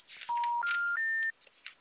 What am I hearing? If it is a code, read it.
Special information tone.